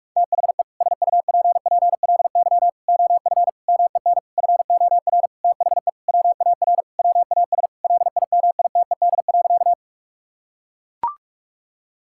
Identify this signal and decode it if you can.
Morse code, heard as a keyed tone, about 43 wpm, 700 Hz. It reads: THE SUPPLY OF MEN FOR THE WAR WAS LIMITED.
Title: Sentences from Top 1000 Words - Review Part 6 - 45wpm